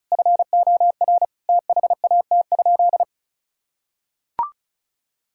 Morse code, heard as a keyed tone, about 34 wpm, 700 Hz. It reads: FOR THAT?
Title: Sentences from Top 1000 Words - Review Part 5 - 35wpm